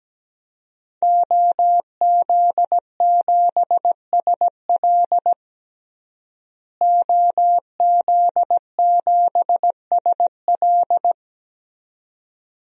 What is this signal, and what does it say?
Morse code, heard as a keyed tone, about 17 wpm, 700 Hz. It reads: OZ7SL OZ7SL
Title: DXCC-100 Call Signs - 5 Characters - Repeated 2x - 17wpm